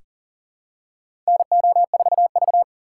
Morse code, heard as a keyed tone, about 40 wpm, 700 Hz. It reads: NO4V
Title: USA Call Signs - 4 Characters - 6x Character Spacing - 40wpm